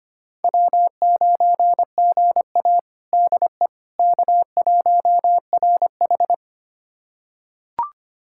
Morse code, heard as a keyed tone, about 25 wpm, 700 Hz. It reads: W9GA DE K1RH
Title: USA Callsign DE Callsign - 4 Characters - 25wpm